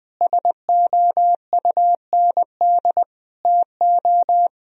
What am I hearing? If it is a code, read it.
Morse code, heard as a keyed tone, about 20 wpm, 700 Hz. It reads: SOUND TO
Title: Sentences from Top 1000 Words - Review Part 1 - 20wpm